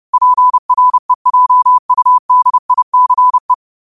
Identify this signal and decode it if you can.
Morse code, heard as a keyed tone, about 30 wpm, 1000 Hz. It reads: PREJUDICE
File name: qotdfiles / 30wpm